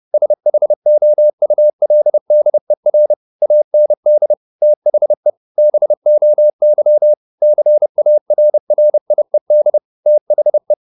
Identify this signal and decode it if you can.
Morse code, heard as a keyed tone, about 30 wpm, 600 Hz. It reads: SHOULDER AND THE BOY CARRIED THE
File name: the old man and the sea / the old man and the sea 1 30wpm